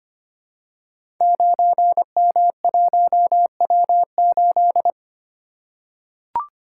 Morse code, heard as a keyed tone, about 25 wpm, 700 Hz. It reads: 9M1W8
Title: Code Groups - 5 Letter:Numbers - 25wpm